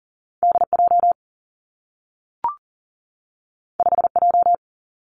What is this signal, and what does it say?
Morse code, heard as a keyed tone, about 43 wpm, 700 Hz. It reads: D J 5 J
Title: Two Challenging Letter-Number Combinations 40wpm